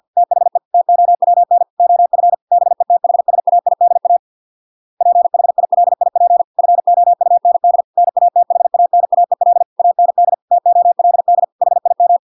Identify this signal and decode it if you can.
Morse code, heard as a keyed tone, about 50 wpm, 700 Hz. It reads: THE TOWN OF BETHSAIDA PHILIP FOUND NATHANAEL AND TOLD HIM